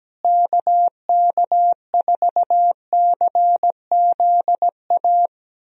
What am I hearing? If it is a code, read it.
Morse code, heard as a keyed tone, about 17 wpm, 700 Hz. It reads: KK4CZA
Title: USA Call Signs - 6 Characters - Repeated 3x - 17wpm